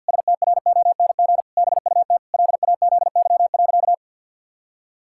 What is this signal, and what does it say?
Morse code, heard as a keyed tone, about 50 wpm, 700 Hz. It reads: STRONG BUT LAZY.
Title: Pangram Challenge v2 - 50wpm